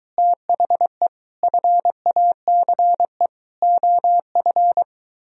Morse code, heard as a keyed tone, about 23 wpm, 700 Hz. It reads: THE FACE OF